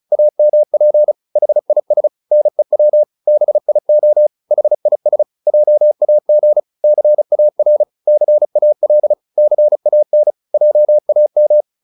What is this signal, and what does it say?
Morse code, heard as a keyed tone, about 34 wpm, 600 Hz. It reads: AMP HIS NEW BIO HIS JAG CAR CAL CAN JAM